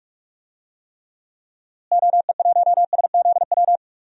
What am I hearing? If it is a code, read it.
Morse code, heard as a keyed tone, about 42 wpm, 700 Hz. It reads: OE1SZW
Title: DXCC-100 Call Signs - 6 Characters - Repeated 3x - 45wpm